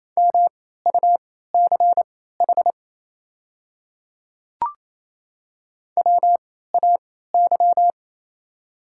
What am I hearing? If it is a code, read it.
Morse code, heard as a keyed tone, about 28 wpm, spaced as for 16 wpm, 700 Hz. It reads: MUCH WAY
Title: 3x Character Spacing - Top 300 Words - 28wpm